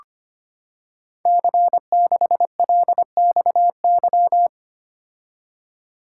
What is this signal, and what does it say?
Morse code, heard as a keyed tone, about 25 wpm, 700 Hz. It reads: C6LXY
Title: Code Groups - 5 Letter:Numbers - 25wpm